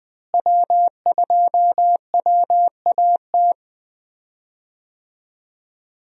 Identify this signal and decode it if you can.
Morse code, heard as a keyed tone, about 19 wpm, 700 Hz. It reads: W2WAT
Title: USA Call Signs - 5 Characters - Repeated 3x - 20wpm